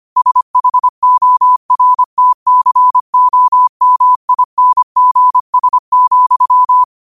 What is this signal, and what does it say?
Morse code, heard as a keyed tone, about 25 wpm, 1000 Hz. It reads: SHORTCOMINGS,
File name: qotdfiles / 25wpm